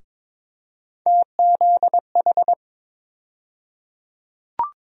Morse code, heard as a keyed tone, about 22 wpm, 700 Hz. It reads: TZH
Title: USA Call Sign Postfixes - 3 Letters 22wpm